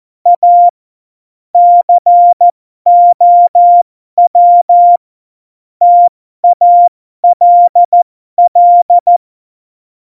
Morse code, heard as a keyed tone, about 14 wpm, spaced as for 12 wpm, 700 Hz. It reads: A COW TALL
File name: Morsekurs / Witze